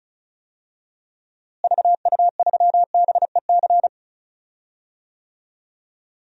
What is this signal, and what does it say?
Morse code, heard as a keyed tone, about 34 wpm, 700 Hz. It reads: VU3BEC
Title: DXCC-100 Call Signs - 6 Characters - Repeated 2x - 35wpm